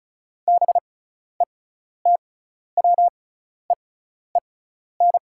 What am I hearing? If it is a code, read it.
Morse code, heard as a keyed tone, about 36 wpm, spaced as for 12 wpm, 690 Hz. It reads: BETWEEN
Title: 6x Character Spacing - Top 300 Words - 35wpm